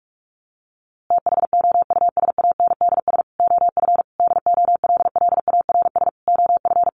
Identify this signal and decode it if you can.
Morse code, heard as a keyed tone, about 44 wpm, 700 Hz. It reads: THOUSANDS OF DOLLARS OF
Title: A Sound Of Thunder 45wpm